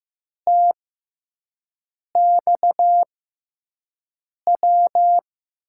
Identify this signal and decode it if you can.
Morse code, heard as a keyed tone, about 15 wpm, spaced as for 5 wpm, 700 Hz. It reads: TXW
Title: USA Call Signs - 6 Characters - 6x Character Spacing - 15wpm